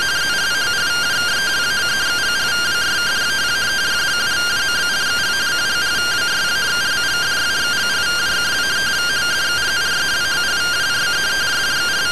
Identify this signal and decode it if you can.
Two-tone teleprinter signal, about 170 Hz shift.